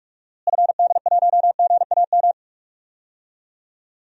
Morse code, heard as a keyed tone, about 46 wpm, 700 Hz. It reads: FD1GAM
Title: DXCC-100 Call Signs - 6 Characters - Repeated 2x - 45wpm